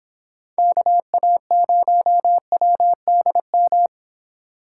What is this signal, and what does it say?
Morse code, heard as a keyed tone, about 31 wpm, 700 Hz. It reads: KA0WDM